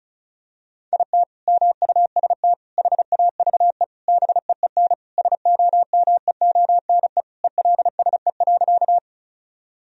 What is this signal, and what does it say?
Morse code, heard as a keyed tone, about 36 wpm, 700 Hz. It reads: IT MUST HAVE BEEN SOMEONE ELSE.
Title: Sentences from Top 1000 Words - Review Part 4 - 35wpm